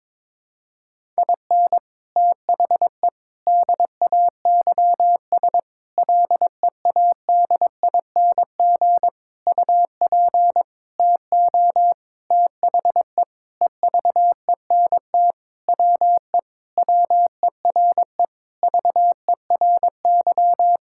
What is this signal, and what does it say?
Morse code, heard as a keyed tone, about 22 wpm, 700 Hz. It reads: IN THE DAYS LEADING UP TO THE EVENT WE WERE VERY
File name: Sentences with 7-Word Phrases - 22wpm